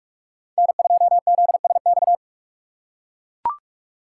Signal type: Morse code

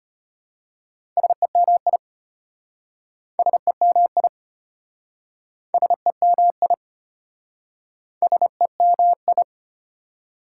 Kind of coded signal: Morse code

